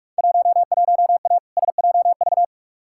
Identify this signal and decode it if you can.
Morse code, heard as a keyed tone, about 44 wpm, 700 Hz. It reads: 11A SJV
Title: Field Day Exchanges Encore - 45wpm